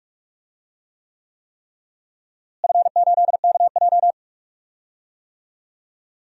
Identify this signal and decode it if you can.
Morse code, heard as a keyed tone, about 43 wpm, 700 Hz. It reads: F8KJ
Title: DXCC-100 Call Signs - 4 Characters - Repeated 3x - 45wpm